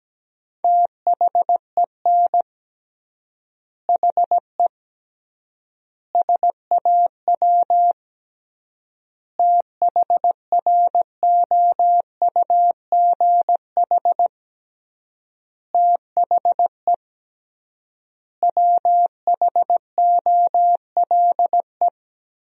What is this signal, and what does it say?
Morse code, heard as a keyed tone, about 17 wpm, 700 Hz. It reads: THEN HE SAW THROUGH THE WHOLE
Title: 3x Word-Spacing - Sentences from Top 400 Words - 17wpm